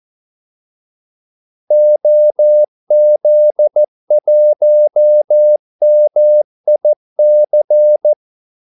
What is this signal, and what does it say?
Morse code, heard as a keyed tone, about 14 wpm, 600 Hz. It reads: OZ1MIC
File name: Callsigns-14wpm